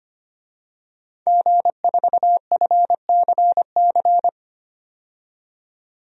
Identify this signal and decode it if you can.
Morse code, heard as a keyed tone, about 25 wpm, 700 Hz. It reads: G4FCC
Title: DXCC-100 Call Signs - 5 Characters - 25wpm